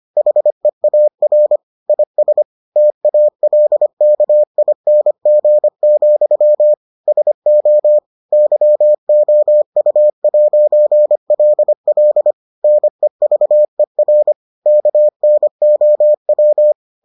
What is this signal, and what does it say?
Morse code, heard as a keyed tone, about 25 wpm, 600 Hz. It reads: HEAR IS TALKING, SO YOU'LL NEVER KNOW